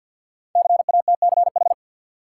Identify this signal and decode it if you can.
Morse code, heard as a keyed tone, about 53 wpm, 700 Hz. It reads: CATCH
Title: Sentences from Top 1500 Words - New Word Drill - 50wpm